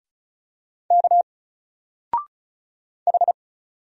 Morse code, heard as a keyed tone, about 35 wpm, 700 Hz. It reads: K H
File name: Single-Letter-Number-35wpm